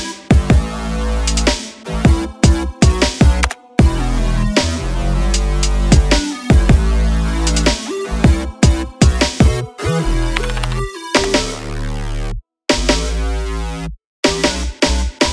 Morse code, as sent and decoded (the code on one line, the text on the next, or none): none